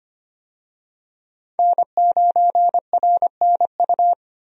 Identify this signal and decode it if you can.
Morse code, heard as a keyed tone, about 25 wpm, 700 Hz. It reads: N9RNU
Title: USA Call Signs - 5 Characters - Repeated 2x - 25wpm